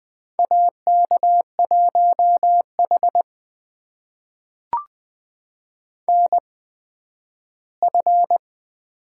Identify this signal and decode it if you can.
Morse code, heard as a keyed tone, about 20 wpm, spaced as for 12 wpm, 700 Hz. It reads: AK1H NF
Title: USA Call Signs - 4 Characters - 8x Character Spacing - 20wpm